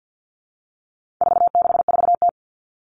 Morse code, heard as a keyed tone, about 50 wpm, 700 Hz. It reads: 464T
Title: All Cut Numbers - 4-Digit 50wpm